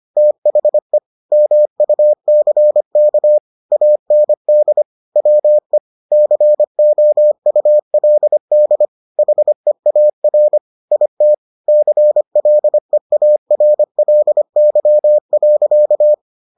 Morse code, heard as a keyed tone, about 25 wpm, 600 Hz. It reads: THE MUCK AND WE COULD HEAR IT CLEARLY.